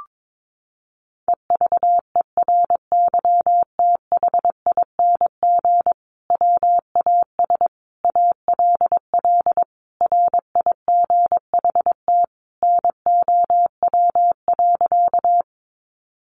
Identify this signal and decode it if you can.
Morse code, heard as a keyed tone, about 22 wpm, 700 Hz. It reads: EVERYTHING WAS ALL RIGHT NOW.
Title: Sentences from Top 700 Words 22wpm